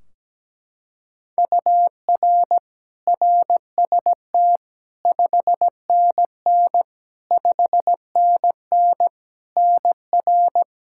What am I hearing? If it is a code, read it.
Morse code, heard as a keyed tone, about 17 wpm, 700 Hz. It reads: UR RST 5NN 5NN NR
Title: Practice QSO - Exchange 1 - Beginning - 17wpm